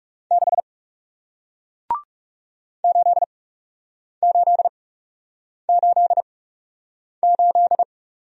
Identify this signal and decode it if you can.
Morse code, heard as a keyed tone, about 40 wpm, spaced as for 8 wpm, 700 Hz. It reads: 6 8888